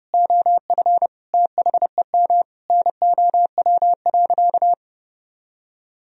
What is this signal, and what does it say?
Morse code, heard as a keyed tone, about 30 wpm, 700 Hz. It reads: OF THEM NOW.
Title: Sentences from Top 1000 Words - Review Part 1 - 30wpm